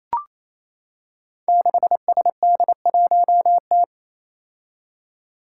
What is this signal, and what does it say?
Morse code, heard as a keyed tone, about 28 wpm, 700 Hz. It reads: 6SD1T
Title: Five Challenging Letter-Number Combinations 28wpm